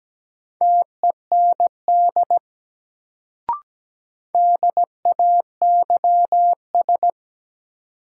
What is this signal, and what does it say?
Morse code, heard as a keyed tone, about 17 wpm, 700 Hz. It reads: TEND DAYS
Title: Sentences from Top 1000 Words - Review Part 5 - New Word Drill - 17wpm